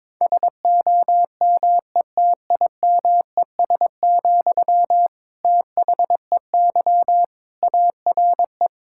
Morse code, heard as a keyed tone, about 22 wpm, 700 Hz. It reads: SOMETIMES, THEY ARE
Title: Sentences from Top 1000 Words - Review Part 5 - 22wpm